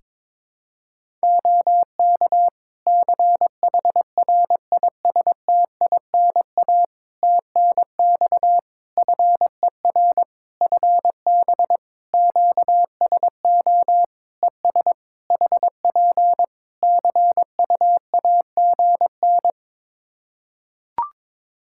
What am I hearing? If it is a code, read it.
Morse code, heard as a keyed tone, about 22 wpm, 700 Hz. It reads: OK CHRISTINA TNX FER FB QSO ES HP CUAGN
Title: Practice QSO - Exchange 4 - Ending - 22wpm